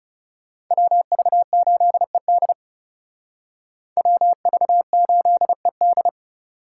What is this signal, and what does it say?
Morse code, heard as a keyed tone, about 36 wpm, 700 Hz. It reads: WV8ED WV8ED
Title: Speed Racing - USA 2x2 Call Signs 30wpm